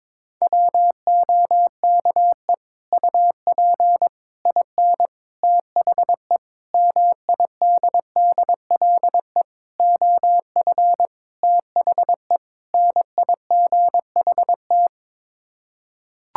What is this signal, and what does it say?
Morse code, heard as a keyed tone, about 22 wpm, 700 Hz. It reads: WOKE UP IN THE MIDDLE OF THE NIGHT
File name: practice / 22wpm / Sets of 8 Words - 22wpm